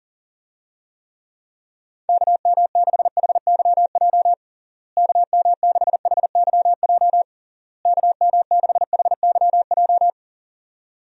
Morse code, heard as a keyed tone, about 40 wpm, 700 Hz. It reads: KM6HYJ KM6HYJ KM6HYJ
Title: USA Call Signs - 6 Characters - Repeated 3x - 40wpm